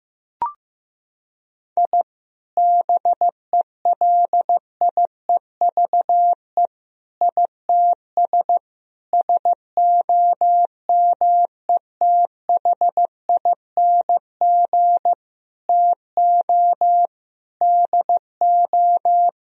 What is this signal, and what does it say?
Morse code, heard as a keyed tone, about 15 wpm, 700 Hz. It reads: I BELIEVE ITS SOMETHING TO DO
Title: Sentences from Top 1000 Words - Review Part 3 - 15wpm